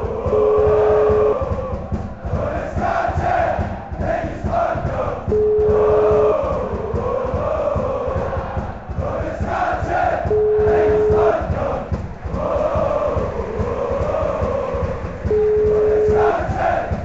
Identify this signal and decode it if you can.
Ringback tone (Europe, 425 Hz), with noise.